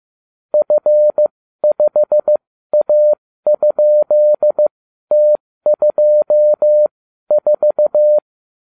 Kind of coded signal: Morse code